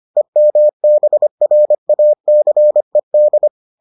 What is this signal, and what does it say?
Morse code, heard as a keyed tone, about 25 wpm, 600 Hz. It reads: EMBRACED